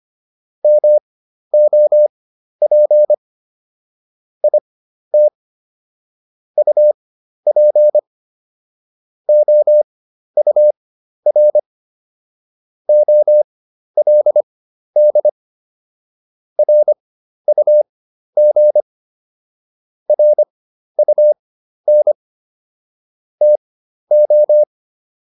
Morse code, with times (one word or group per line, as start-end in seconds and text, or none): 0.6-3.1 MOP
4.4-5.3 IT
6.6-8.0 UP
9.3-11.6 OUR
12.9-15.3 OLD
16.6-18.8 RUG
20.1-22.1 RUN
23.4-24.6 TO